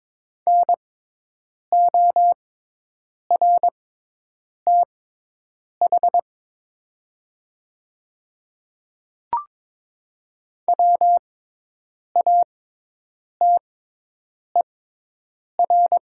Morse code, heard as a keyed tone, about 22 wpm, spaced as for 8 wpm, 700 Hz. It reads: NORTH WATER
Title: 6x Character Spacing - Top 500 Words - 22wpm